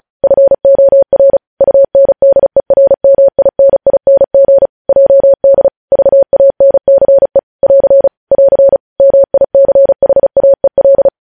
Morse code, heard as a keyed tone, about 36 wpm, 550 Hz. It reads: FOR UNDERMINING JD VANCE <AR> <AR> MICHAEL
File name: podcast / News headlines-1x-35x35